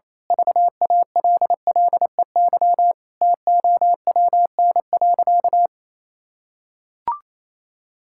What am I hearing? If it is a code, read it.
Morse code, heard as a keyed tone, about 28 wpm, 700 Hz. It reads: VALLEY TOWN.
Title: Pangram Challenge v2 - 28wpm